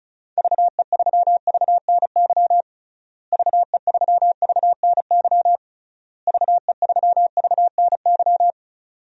Morse code, heard as a keyed tone, about 36 wpm, 700 Hz. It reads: VE3VNY VE3VNY VE3VNY
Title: DXCC-100 Call Signs - 6 Characters - Repeated 3x - 35wpm